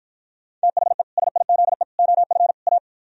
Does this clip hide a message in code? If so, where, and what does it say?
Yes: Morse code from 0.6 to 2.8 s, heard as a keyed tone, about 53 wpm, 700 Hz. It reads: THE SIZE OF A